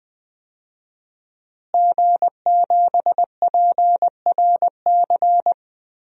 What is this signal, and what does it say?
Morse code, heard as a keyed tone, about 20 wpm, 700 Hz. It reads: G7PRC
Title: DXCC-100 Call Signs - 5 Characters - Repeated 2x - 20wpm